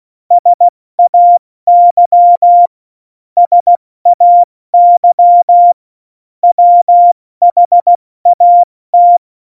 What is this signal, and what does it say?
Morse code, heard as a keyed tone, about 16 wpm, spaced as for 14 wpm, 700 Hz. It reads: SAY SAY WHAT
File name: Morsekurs / Witze / Witz 25 Juli 17, Tempo 70 F80